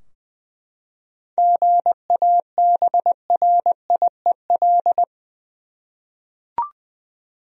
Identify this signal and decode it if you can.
Morse code, heard as a keyed tone, about 20 wpm, 700 Hz. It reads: GABRIEL
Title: USA Common Names - 20wpm